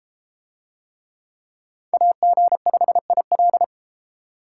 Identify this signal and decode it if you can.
Morse code, heard as a keyed tone, about 33 wpm, 700 Hz. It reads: AG5IL